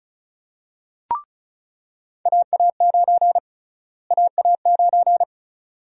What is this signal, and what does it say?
Morse code, heard as a keyed tone, about 36 wpm, 700 Hz. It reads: AA9 AA9
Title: USA Call Sign Prefixes - Repeated 2x - 35wpm